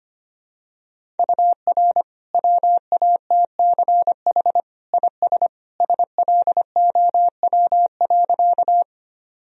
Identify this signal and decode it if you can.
Morse code, heard as a keyed tone, about 25 wpm, 700 Hz. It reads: UR WATCH IS SLOW.
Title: Sentences from Top 400 Words and Abbreviations 25wpm